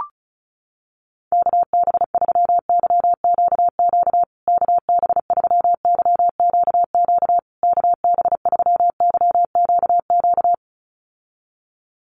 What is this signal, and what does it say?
Morse code, heard as a keyed tone, about 34 wpm, 700 Hz. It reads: KB3YQQ KB3YQQ KB3YQQ
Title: USA Call Signs Repeated 3x 35wpm